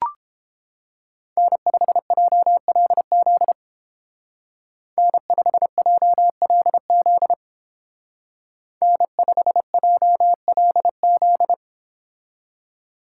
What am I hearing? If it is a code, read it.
Morse code, heard as a keyed tone, about 31 wpm, 700 Hz. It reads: N5JLZ N5JLZ N5JLZ